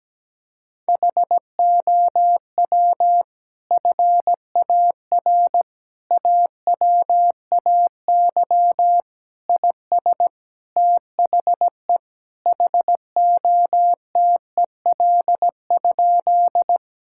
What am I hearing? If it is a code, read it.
Morse code, heard as a keyed tone, about 17 wpm, 700 Hz. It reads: HOW FAR AWAY IS THE HOTEL?
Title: Sentences from Top 1600 Words 17wpm